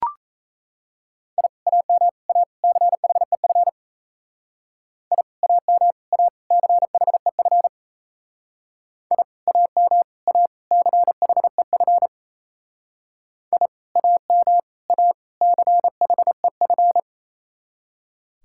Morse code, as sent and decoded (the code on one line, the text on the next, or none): .. / .- -- / .- / -.-. .... ..-. / .. / .- -- / .- / -.-. .... ..-. / .. / .- -- / .- / -.-. .... ..-. / .. / .- -- / .- / -.-. .... ..-.
I AM A CHEF I AM A CHEF I AM A CHEF I AM A CHEF